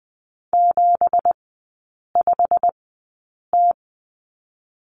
Morse code, heard as a keyed tone, about 20 wpm, spaced as for 8 wpm, 700 Hz. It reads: 75T
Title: 166 Lesson 27e - Number 7 - Three Characters 20wpm